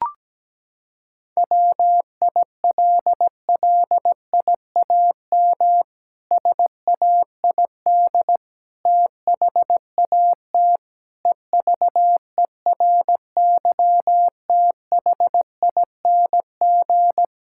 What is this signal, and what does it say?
Morse code, heard as a keyed tone, about 17 wpm, 700 Hz. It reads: WILLIAM SAID THAT EVERYTHING
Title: Pangram Challenge v2 - 17wpm